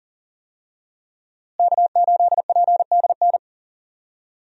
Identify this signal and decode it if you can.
Morse code, heard as a keyed tone, about 40 wpm, 700 Hz. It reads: K8PDN
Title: USA Call Signs - 5 Characters - Repeated 2x - 40wpm